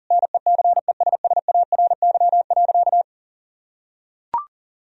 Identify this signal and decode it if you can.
Morse code, heard as a keyed tone, about 40 wpm, 700 Hz. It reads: NECESSARY.
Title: Sentences from Top 1000 Words - Review Part 1 - 40wpm